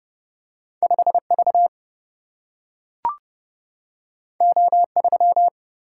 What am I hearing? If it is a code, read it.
Morse code, heard as a keyed tone, about 30 wpm, 700 Hz. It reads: 5V O3
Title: Single-Letter Single-Number 30wpm